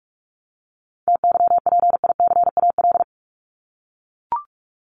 Morse code, heard as a keyed tone, about 44 wpm, 700 Hz. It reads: TYPICAL
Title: Top 1001-1250 Words 45wpm